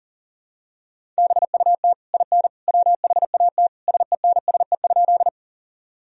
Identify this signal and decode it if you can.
Morse code, heard as a keyed tone, about 40 wpm, 700 Hz. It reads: BUT IN WHAT SENSE?
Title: Sentences from Top 1000 Words - Review Part 4 - 40wpm